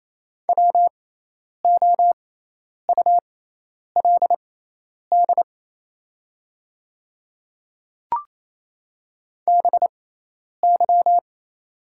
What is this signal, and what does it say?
Morse code, heard as a keyed tone, about 28 wpm, spaced as for 10 wpm, 700 Hz. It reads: WOULD BY
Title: 6x Character Spacing - Top 100 Words - 28wpm